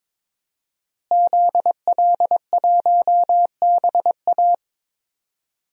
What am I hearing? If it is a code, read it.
Morse code, heard as a keyed tone, about 22 wpm, 700 Hz. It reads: ZL1BA
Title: DXCC-100 Call Signs - 5 Characters - Repeated 2x - 22wpm